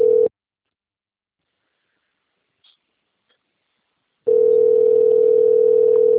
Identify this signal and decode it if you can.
Ringback tone (North America), 440 + 480 Hz.
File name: rg-1600-unknown- - - .20697